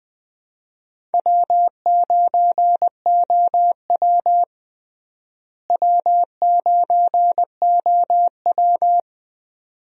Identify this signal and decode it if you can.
Morse code, heard as a keyed tone, about 20 wpm, 700 Hz. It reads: W9OW W9OW
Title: USA Call Signs - 4 Characters - Repeated 2x - 20wpm